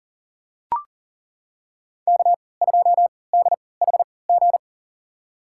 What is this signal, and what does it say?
Morse code, heard as a keyed tone, about 40 wpm, spaced as for 23 wpm, 700 Hz. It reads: K2DHG
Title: USA Call Signs - 5 Characters - 3x Character Spacing - 40wpm